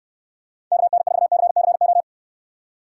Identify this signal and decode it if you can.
Morse code, heard as a keyed tone, about 69 wpm, 700 Hz. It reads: KN4CCO